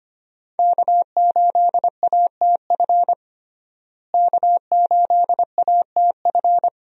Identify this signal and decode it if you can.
Morse code, heard as a keyed tone, about 25 wpm, 700 Hz. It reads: K8ATF K8ATF
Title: USA Call Signs - 5 Characters - Repeated 2x - 25wpm